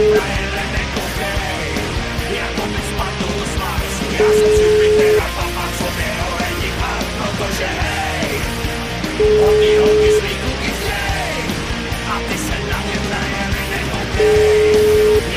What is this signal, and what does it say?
Ringback tone (Europe, 425 Hz), with noise.